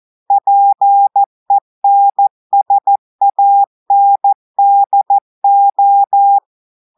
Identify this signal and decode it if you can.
Morse code, heard as a keyed tone, about 14 wpm, 800 Hz. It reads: PENSANDO
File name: audios / portugais / portugais texte en clair 2 vitesse 14 mots minutes